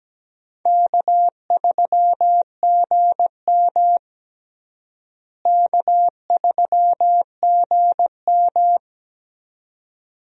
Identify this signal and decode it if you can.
Morse code, heard as a keyed tone, about 17 wpm, 700 Hz. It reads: K3GM K3GM
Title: USA Call Signs - 4 Characters - Repeated 2x - 17wpm